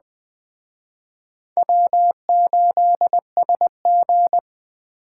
Morse code, heard as a keyed tone, about 20 wpm, 700 Hz. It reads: W8SG
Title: Speed Racing - USA 1x2 Call Signs 17wpm